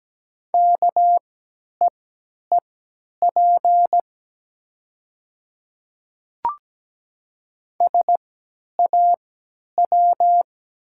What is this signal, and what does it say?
Morse code, heard as a keyed tone, about 17 wpm, spaced as for 10 wpm, 690 Hz. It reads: KEEP SAW
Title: 3x Character Spacing - Top 500 Words - 17wpm